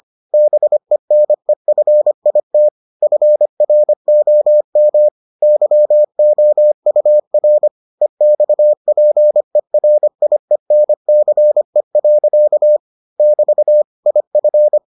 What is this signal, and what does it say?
Morse code, heard as a keyed tone, about 25 wpm, 600 Hz. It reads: BENEFIT FROM YOUR EXPERIENCE. <BT> IF